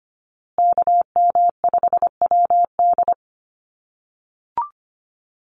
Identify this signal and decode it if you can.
Morse code, heard as a keyed tone, about 25 wpm, 700 Hz. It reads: KM5WD
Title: USA Call Signs Repeated 2x 25wpm